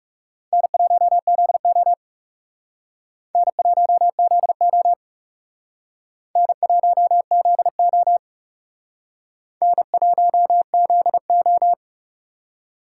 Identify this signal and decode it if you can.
Morse code, heard as a keyed tone, about 36 wpm, 700 Hz. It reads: N1ZO N1ZO N1ZO N1ZO
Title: Speed Racing - USA 1x2 Call Signs 30wpm